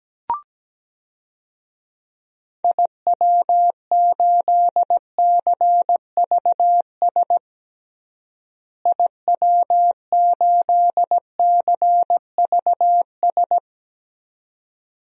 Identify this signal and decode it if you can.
Morse code, heard as a keyed tone, about 17 wpm, 700 Hz. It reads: IW8CVS IW8CVS
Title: DXCC-100 Call Signs - 6 Characters - Repeated 2x - 17wpm